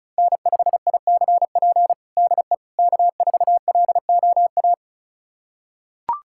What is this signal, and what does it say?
Morse code, heard as a keyed tone, about 36 wpm, 700 Hz. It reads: N5ICP DE K4LOA
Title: USA Callsign DE Callsign - 5 Characters - 35wpm